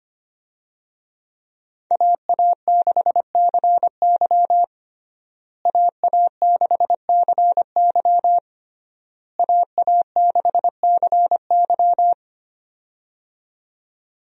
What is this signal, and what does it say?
Morse code, heard as a keyed tone, about 25 wpm, 700 Hz. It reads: AA6CY AA6CY AA6CY
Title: USA Call Signs - 5 Characters - Repeated 3x - 25wpm